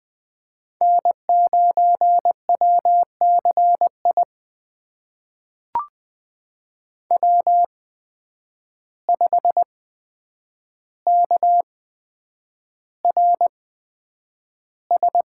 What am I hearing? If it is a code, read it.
Morse code, heard as a keyed tone, about 20 wpm, spaced as for 9 wpm, 700 Hz. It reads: N9WCI W5KRS
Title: USA Call Signs - 5 Characters - 8x Character Spacing - 20wpm